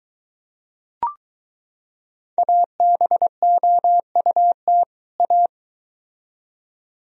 Morse code, heard as a keyed tone, about 22 wpm, 700 Hz. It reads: ABOUT A